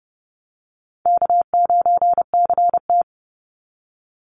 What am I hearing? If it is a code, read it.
Morse code, heard as a keyed tone, about 30 wpm, 700 Hz. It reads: K9CT
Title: Easy Call Signs Repeated 3x - 30wpm